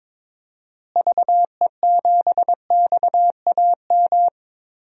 Morse code, heard as a keyed tone, about 21 wpm, 700 Hz. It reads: VE7XAM
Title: DXCC-100 Call Signs - 6 Characters - Repeated 3x - 22wpm